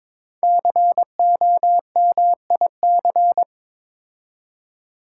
Morse code, heard as a keyed tone, about 22 wpm, 700 Hz. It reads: COMIC
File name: 5 Letter Words - First 3 Letters Given - 22wpm